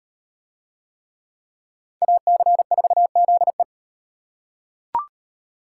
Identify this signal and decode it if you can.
Morse code, heard as a keyed tone, about 38 wpm, 700 Hz. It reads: AC4ZE